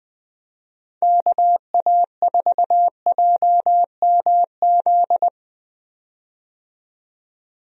Morse code, heard as a keyed tone, about 20 wpm, 700 Hz. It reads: KA4JMZ
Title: USA Call Signs - 6 Characters - 20wpm